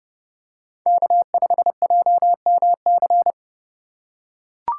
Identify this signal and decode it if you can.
Morse code, heard as a keyed tone, about 32 wpm, 700 Hz. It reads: K5JMC
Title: USA Call Signs - 5 Characters - 2x Character Spacing - 30wpm